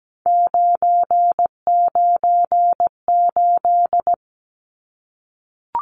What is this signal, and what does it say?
Morse code, heard as a keyed tone, about 17 wpm, 700 Hz. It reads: 998
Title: Numbers - 3-Digit 17wpm